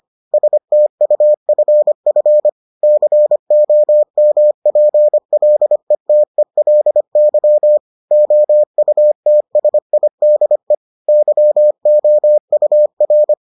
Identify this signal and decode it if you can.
Morse code, heard as a keyed tone, about 25 wpm, 600 Hz. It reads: STUFF COMPLETELY OUTSIDE YOUR